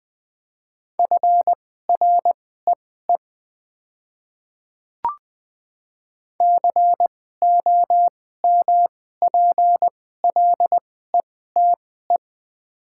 Morse code, heard as a keyed tone, about 20 wpm, spaced as for 14 wpm, 700 Hz. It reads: FREE COMPLETE
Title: 2x Character Spacing - Top 500 Words - 20wpm